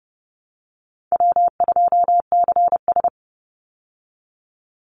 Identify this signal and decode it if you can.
Morse code, heard as a keyed tone, about 30 wpm, 700 Hz. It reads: W2CS
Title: Easy Call Signs Repeated 3x - 30wpm